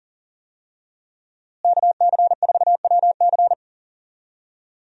Morse code, heard as a keyed tone, about 40 wpm, 700 Hz. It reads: KC4WC
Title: USA Call Signs - 5 Characters - Repeated 3x - 40wpm